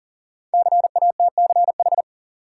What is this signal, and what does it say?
Morse code, heard as a keyed tone, about 40 wpm, 700 Hz. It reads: CATCH
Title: Sentences from Top 1500 Words - New Word Drill Alt - 40wpm